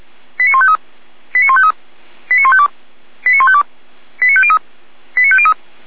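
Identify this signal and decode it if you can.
Five-tone selective call (ZVEI), heard as quick runs of five tones, tones of about 70 ms. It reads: 89153 89153 89152 89152 89503 89503
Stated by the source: Received by SDR.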